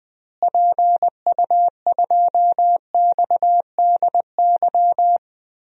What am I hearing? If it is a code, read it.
Morse code, heard as a keyed tone, about 20 wpm, 700 Hz. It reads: PU2XDY
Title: DXCC-100 Call Signs - 6 Characters - Repeated 2x - 20wpm